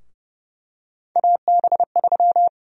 Morse code, heard as a keyed tone, about 32 wpm, 700 Hz. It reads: AB3
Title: USA Call Sign Prefixes - Repeated 2x - 30wpm